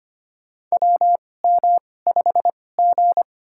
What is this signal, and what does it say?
Morse code, heard as a keyed tone, about 25 wpm, spaced as for 18 wpm, 700 Hz. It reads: WM5G
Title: USA Call Signs - 2x Character Spacing - 25wpm